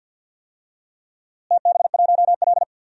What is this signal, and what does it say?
Morse code, heard as a keyed tone, about 52 wpm, 700 Hz. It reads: TB1L
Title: DXCC-100 Call Signs - 4 Characters - Repeated 2x - 50wpm